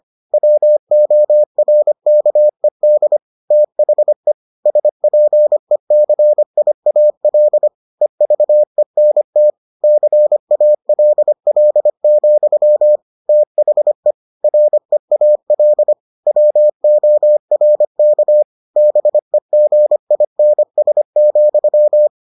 Morse code, heard as a keyed tone, about 25 wpm, 600 Hz. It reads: WORKED THE SPECIAL EVENT CALL, THE REAL WORK BEGINS,